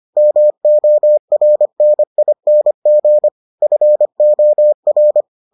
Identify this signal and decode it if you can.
Morse code, heard as a keyed tone, about 25 wpm, 600 Hz. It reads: MORNING FOR